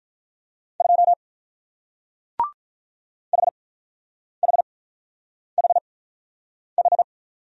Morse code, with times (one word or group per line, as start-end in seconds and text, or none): 0.8-1.1 2
3.3-7.0 HHHH